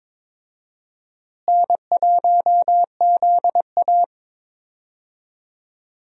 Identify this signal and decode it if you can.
Morse code, heard as a keyed tone, about 22 wpm, 700 Hz. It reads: N1ZA
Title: USA Call Signs - 4 Characters - Repeated 2x - 22wpm